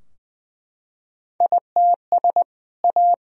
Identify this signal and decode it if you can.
Morse code, heard as a keyed tone, about 20 wpm, 700 Hz. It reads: ITS A
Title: Sentences from Top 1000 Words - Review Part 3 - 20wpm